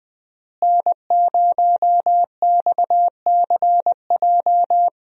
Morse code, heard as a keyed tone, about 21 wpm, 700 Hz. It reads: N0XCJ
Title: USA Call Signs - 5 Characters - Repeated 3x - 20wpm